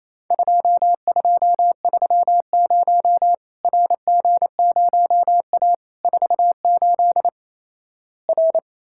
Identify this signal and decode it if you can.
Morse code, heard as a keyed tone, about 28 wpm, 700 Hz. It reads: 2230 RG0A 48 R
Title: Worked All Europe DX Contest - QTC Exchange - 28wpm